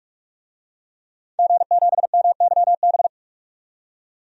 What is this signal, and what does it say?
Morse code, heard as a keyed tone, about 46 wpm, 700 Hz. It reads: G7MYB
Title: DXCC-100 Call Signs - 5 Characters - 45wpm